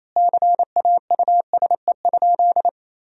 Morse code, heard as a keyed tone, about 28 wpm, 700 Hz. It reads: CAUSE?
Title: 2x Word-Spacing - Sentences from Top 200 Words - 28wpm